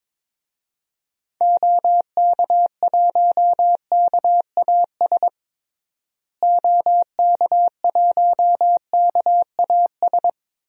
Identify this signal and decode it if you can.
Morse code, heard as a keyed tone, about 22 wpm, 700 Hz. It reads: OK1KAS OK1KAS
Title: DXCC-100 Call Signs - 6 Characters - Repeated 3x - 22wpm